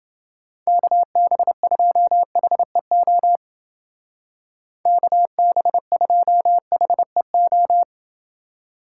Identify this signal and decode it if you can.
Morse code, heard as a keyed tone, about 29 wpm, 700 Hz. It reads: KB2HEO KB2HEO